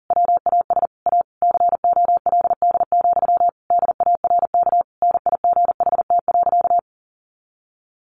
Morse code, heard as a keyed tone, about 37 wpm, 700 Hz. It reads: WAS A COLD, DARK NIGHT.
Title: Sentences from Top 500 Words 40wpm